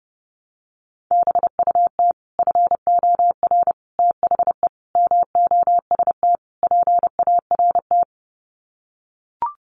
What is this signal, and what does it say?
Morse code, heard as a keyed tone, about 30 wpm, 700 Hz. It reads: BUT FOR THE MOST PART
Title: Sets of 5 Words Encore 30wpm